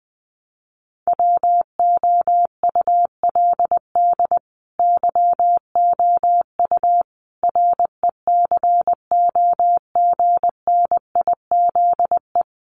Morse code, heard as a keyed tone, about 20 wpm, 700 Hz. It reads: WOULD YOU RECOGNIZE
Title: Sentences from Top 1400 Words 20wpm